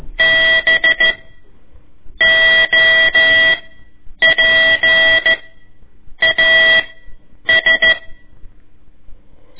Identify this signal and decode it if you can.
Morse code, heard as a keyed tone, about 13 wpm, spaced as for 7 wpm, 1950 Hz. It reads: BOPAS
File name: ik1weg2 / CORSI CW / lez4-1